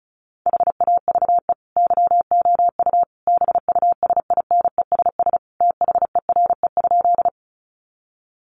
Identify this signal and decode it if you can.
Morse code, heard as a keyed tone, about 34 wpm, 700 Hz. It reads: HAVE YOU BUSINESS THERE?
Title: Sentences from Top 600 Words 35wpm